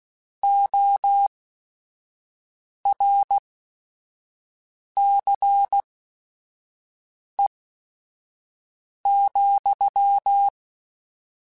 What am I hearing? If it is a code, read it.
Morse code, heard as a keyed tone, about 16 wpm, spaced as for 5 wpm, 800 Hz. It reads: ORCE,
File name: Technical Subjects / Technical Subjects-mp3s / XL @5